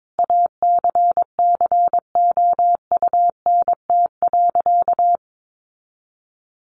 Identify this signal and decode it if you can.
Morse code, heard as a keyed tone, about 22 wpm, 700 Hz. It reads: ACCOUNT.
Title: Sentences from Top 900 Words 22wpm